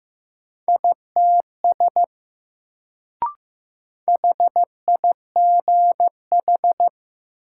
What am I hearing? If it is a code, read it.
Morse code, heard as a keyed tone, about 15 wpm, 700 Hz. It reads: ITS HIGH
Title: Sets of 2 Words - New Word Drill - 15wpm